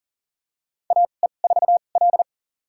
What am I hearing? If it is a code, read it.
Morse code, heard as a keyed tone, about 39 wpm, spaced as for 29 wpm, 700 Hz. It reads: AE4L
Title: USA Call Signs - 4 Characters - 2x Character Spacing - 40wpm